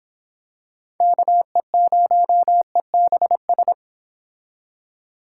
Morse code, heard as a keyed tone, about 26 wpm, 700 Hz. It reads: KE0EBS